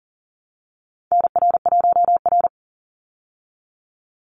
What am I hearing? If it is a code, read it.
Morse code, heard as a keyed tone, about 40 wpm, 700 Hz. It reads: NR1R
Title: Call Signs Repeated 2x - 40wpm